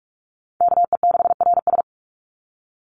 Morse code, heard as a keyed tone, about 44 wpm, 700 Hz. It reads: KE6RS
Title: NA SOTA Chaser Callsigns 45wpm